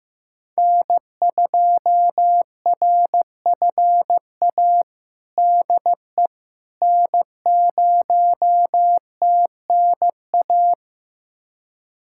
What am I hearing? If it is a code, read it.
Morse code, heard as a keyed tone, about 16 wpm, 700 Hz. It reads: N2RFA DE N0TNA
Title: USA Callsign DE Callsign - 5 Characters - 15wpm